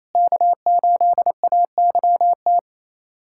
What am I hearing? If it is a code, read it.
Morse code, heard as a keyed tone, about 28 wpm, 700 Hz. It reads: K8AYT
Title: USA Callsign DE Callsign 28wpm